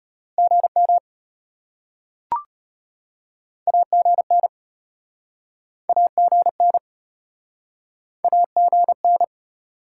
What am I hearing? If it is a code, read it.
Morse code, heard as a keyed tone, about 33 wpm, 700 Hz. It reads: GM AGN AGN AGN